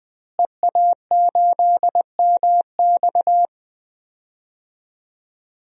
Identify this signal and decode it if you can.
Morse code, heard as a keyed tone, about 20 wpm, 700 Hz. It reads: EA8MX
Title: DXCC-100 Call Signs - 5 Characters - Repeated 3x - 20wpm